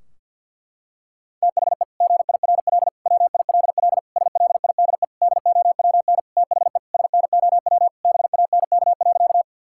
Morse code, heard as a keyed tone, about 50 wpm, 700 Hz. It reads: THE GIRL WILL SLIDE DOWN THE SNOW BANK.
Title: Sentences from Top 1600 Words 50wpm